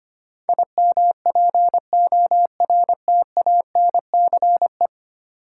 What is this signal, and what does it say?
Morse code, heard as a keyed tone, about 25 wpm, 700 Hz. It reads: IMPORTANCE
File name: 10 Letter Words - First 3 Letters Given - 25wpm